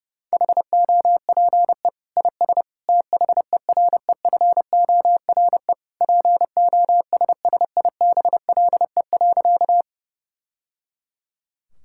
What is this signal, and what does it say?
Morse code, heard as a keyed tone, about 30 wpm, 700 Hz. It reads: HOPE IS THEREFORE POSSIBLE.
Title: Sentences from Top 1000 Words - Review Part 5 - 30wpm